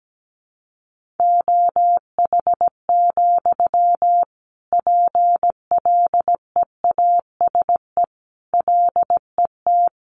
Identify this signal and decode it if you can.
Morse code, heard as a keyed tone, about 17 wpm, 700 Hz. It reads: OH, PLEASE LET
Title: Sentences from Top 1500 Words 17wpm